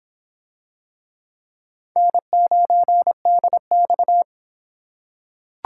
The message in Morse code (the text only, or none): N9DX